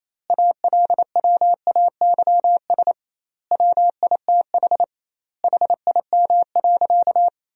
Morse code, heard as a keyed tone, about 28 wpm, 700 Hz. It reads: ALWAYS WITH HIM.
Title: 2x Word-Spacing - Sentences from Top 400 Words - 28wpm